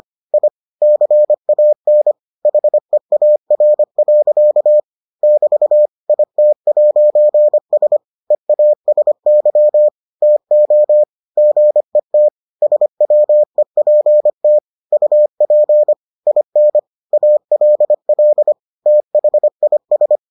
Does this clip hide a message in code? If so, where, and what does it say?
Yes: Morse code from 0.3 to 20.2 s, heard as a keyed tone, about 25 wpm, 600 Hz. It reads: I CAN HEAR. <BT> IT'S EASY TO GET SWEPT UP IN ALL THIS